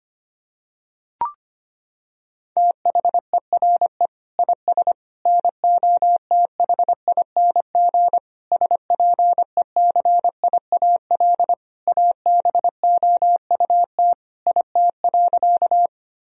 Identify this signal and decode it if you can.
Morse code, heard as a keyed tone, about 25 wpm, 700 Hz. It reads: THERE IS NOTHING SPECIAL ABOUT IT.
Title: Sentences from Top 1000 Words - Review Part 3 - 25wpm